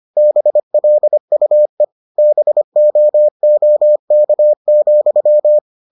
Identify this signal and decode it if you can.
Morse code, heard as a keyed tone, about 25 wpm, 600 Hz. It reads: BLUE BOOK,